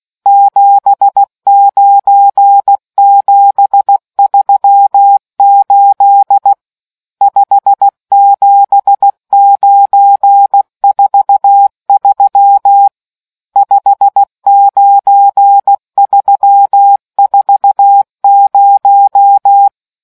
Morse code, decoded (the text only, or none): 79738 57943 59340